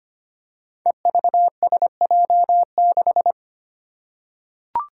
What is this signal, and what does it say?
Morse code, heard as a keyed tone, about 25 wpm, 700 Hz. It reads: EVSJ6
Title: Five Challenging Letter-Number Combinations 25wpm